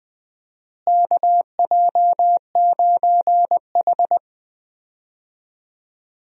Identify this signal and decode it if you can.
Morse code, heard as a keyed tone, about 20 wpm, 700 Hz. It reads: KJ9H
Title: USA Call Signs - 4 Characters - Repeated 2x - 20wpm